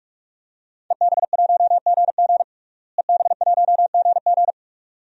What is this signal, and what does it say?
Morse code, heard as a keyed tone, about 41 wpm, 700 Hz. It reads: EB1GG EB1GG
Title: DXCC-100 Call Signs - 5 Characters - Repeated 2x - 45wpm